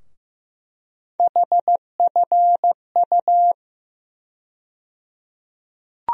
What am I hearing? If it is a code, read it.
Morse code, heard as a keyed tone, about 15 wpm, 700 Hz. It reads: HFU
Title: USA Call Sign Postfixes - 3 Letters 15wpm